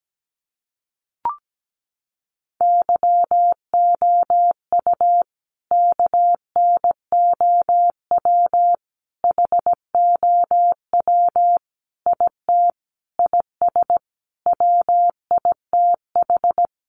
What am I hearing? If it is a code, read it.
Morse code, heard as a keyed tone, about 17 wpm, 700 Hz. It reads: YOU KNOW HOW IT IS WITH
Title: Sentences from Top 100 Words 17wpm